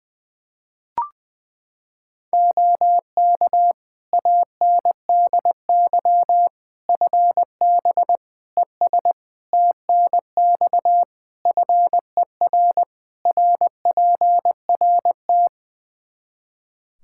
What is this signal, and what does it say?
Morse code, heard as a keyed tone, about 20 wpm, 700 Hz. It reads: OK ANDY FB ES TNX FER RPRT
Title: K1USN SST QSO Practice 20wpm